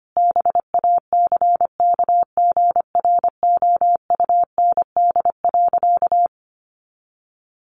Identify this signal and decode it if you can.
Morse code, heard as a keyed tone, about 25 wpm, 700 Hz. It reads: BACKGROUND.
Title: Sentences from Top 1300 Words 25wpm